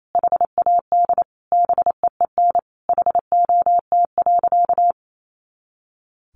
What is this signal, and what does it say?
Morse code, heard as a keyed tone, about 28 wpm, 700 Hz. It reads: HAD BEEN HOT.
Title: Sentences from Top 500 Words 28wpm